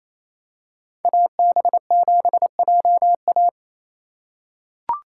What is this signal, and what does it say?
Morse code, heard as a keyed tone, about 28 wpm, 700 Hz. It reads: AB7JA
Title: USA Call Signs - 5 Characters - 2x Character Spacing - 28wpm